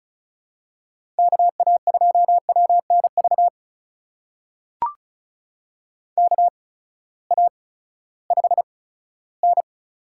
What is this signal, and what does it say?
Morse code, heard as a keyed tone, about 36 wpm, spaced as for 18 wpm, 700 Hz. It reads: KA2WNV KA5N
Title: USA Call Signs - 6 Characters - 8x Character Spacing - 35wpm